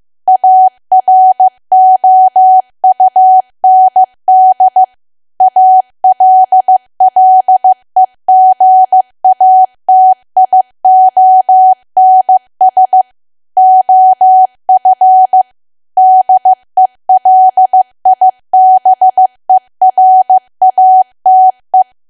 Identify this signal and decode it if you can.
Morse code, heard as a keyed tone, about 15 wpm, 750 Hz. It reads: AROUND ALLEGATIONS OF DELIBERATE
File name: cwa / W1AW-Level2 / 15WPM